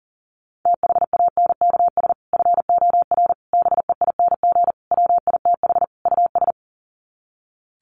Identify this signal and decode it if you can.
Morse code, heard as a keyed tone, about 40 wpm, 700 Hz. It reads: THANKS FOR BEING WITH US
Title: Sets of 5 Words 40wpm